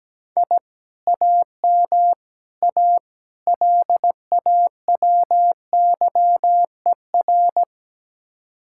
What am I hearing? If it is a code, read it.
Morse code, heard as a keyed tone, about 17 wpm, 700 Hz. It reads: I AM A LAWYER
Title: Speed-Racing Common QSO Phrases Encore - 17wpm